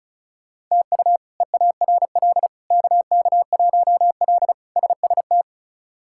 Morse code, heard as a keyed tone, about 36 wpm, 700 Hz. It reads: TU EARL KK1L SST
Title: K1USN SST QSO Practice - 35wpm